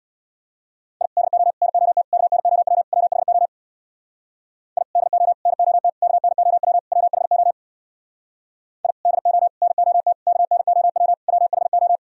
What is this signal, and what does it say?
Morse code, heard as a keyed tone, about 64 wpm, 700 Hz. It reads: I DO NOT KNOW WHO I DO NOT KNOW WHO I DO NOT KNOW WHO